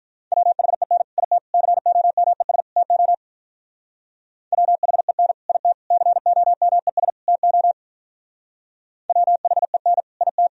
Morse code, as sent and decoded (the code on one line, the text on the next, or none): .-- .... -. / .. - / -.-. --- -- . ... / - --- / .-- .... -. / .. - / -.-. --- -- . ... / - --- / .-- .... -. / .. -
WHEN IT COMES TO WHEN IT COMES TO WHEN IT